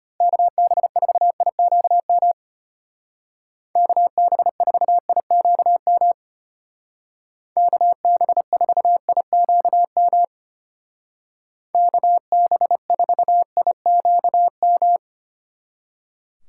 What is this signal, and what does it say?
Morse code, heard as a keyed tone, about 30 wpm, 700 Hz. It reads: KB4IQM KB4IQM KB4IQM KB4IQM
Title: Speed Racing - USA 2x3 Call Signs 25wpm